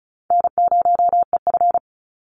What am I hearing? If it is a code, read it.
Morse code, heard as a keyed tone, about 34 wpm, 700 Hz. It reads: N0EF
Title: Easy Call Signs Repeated 3x - 35wpm